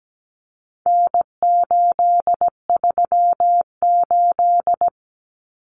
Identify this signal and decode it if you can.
Morse code, heard as a keyed tone, about 17 wpm, 700 Hz. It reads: N838
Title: Common Cut Numbers - 4-Digit 17wpm